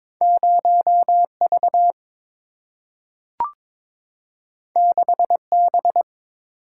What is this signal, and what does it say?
Morse code, heard as a keyed tone, about 22 wpm, 690 Hz. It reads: 0V 6B
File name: practice / 22wpm / Two Letter-Number Combinations - 22wpm